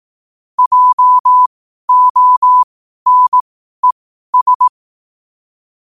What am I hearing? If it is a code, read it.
Morse code, heard as a keyed tone, about 18 wpm, spaced as for 12 wpm, 1000 Hz. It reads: JONES
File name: qotdfiles / 12wpm